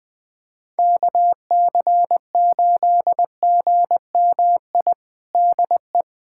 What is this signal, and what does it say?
Morse code, heard as a keyed tone, about 19 wpm, 700 Hz. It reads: KC8GMI DE
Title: USA Callsign DE Callsign - 6 Characters - 20wpm